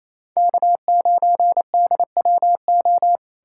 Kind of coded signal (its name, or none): Morse code